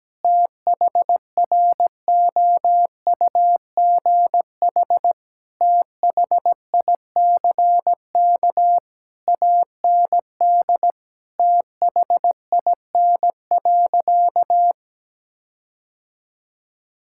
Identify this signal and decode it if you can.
Morse code, heard as a keyed tone, about 17 wpm, 700 Hz. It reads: THROUGH THICK AND THIN.
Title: Sentences from Top 1600 Words 17wpm